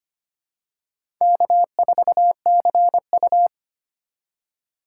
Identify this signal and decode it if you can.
Morse code, heard as a keyed tone, about 25 wpm, 700 Hz. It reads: K4CU
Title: USA Call Signs - 4 Characters - 2x Character Spacing - 25wpm